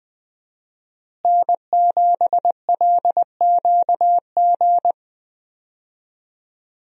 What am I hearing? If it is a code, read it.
Morse code, heard as a keyed tone, about 20 wpm, 700 Hz. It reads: N7LQG
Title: USA Call Signs - 5 Characters - Repeated 3x - 20wpm